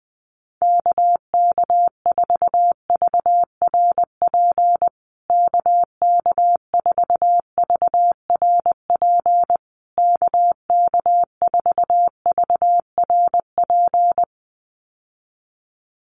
Morse code, heard as a keyed tone, about 20 wpm, 700 Hz. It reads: KK4VRP KK4VRP KK4VRP
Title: USA Call Signs Repeated 3x 20wpm